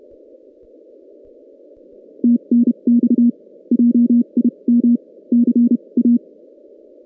Morse code, heard as a keyed tone, about 31 wpm, spaced as for 26 wpm, 240 Hz. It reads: TNX JIM CA
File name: 2016 JAN CW NAQP / 2016 Jan naqp k5ot - 0-7